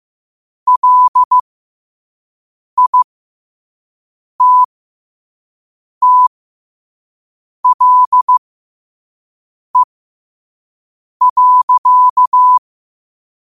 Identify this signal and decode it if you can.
Morse code, heard as a keyed tone, about 15 wpm, spaced as for 5 wpm, 1000 Hz. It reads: LITTLE.
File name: qotdfiles / 5wpm